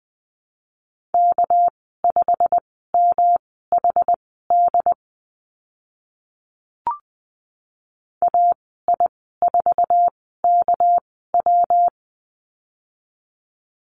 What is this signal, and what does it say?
Morse code, heard as a keyed tone, about 20 wpm, spaced as for 15 wpm, 700 Hz. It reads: K5MHD AI4KW
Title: 200 Lesson 32g - Letter k - Call Signs 20wpm